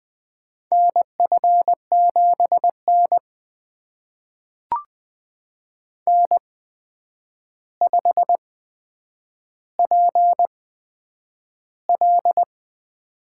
Morse code, heard as a keyed tone, about 20 wpm, spaced as for 9 wpm, 700 Hz. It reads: NF7N N5PL
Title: USA Call Signs - 4 Characters - 8x Character Spacing - 20wpm